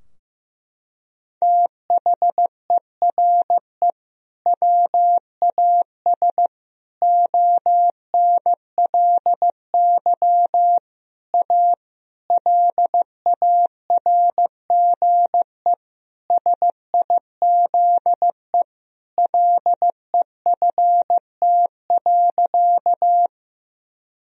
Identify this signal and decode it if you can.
Morse code, heard as a keyed tone, about 15 wpm, 700 Hz. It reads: THERE WAS ONLY A LARGE SIZE LEFT.
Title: Sentences from Top 1000 Words - Review Part 2 - 15wpm